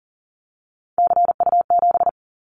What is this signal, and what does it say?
Morse code, heard as a keyed tone, about 40 wpm, 700 Hz. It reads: CU7
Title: Callsign Prefixes for Top 100 DXCC - 40wpm